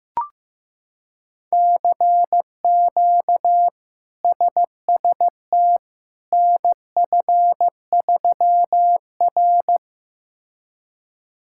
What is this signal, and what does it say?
Morse code, heard as a keyed tone, about 16 wpm, 700 Hz. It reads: CQ SST NF3R
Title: K1USN SST QSO Practice - 15wpm